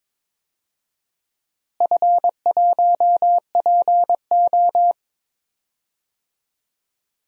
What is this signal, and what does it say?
Morse code, heard as a keyed tone, about 22 wpm, 700 Hz. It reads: F1PO
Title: DXCC-100 Call Signs - 4 Characters - Repeated 2x - 22wpm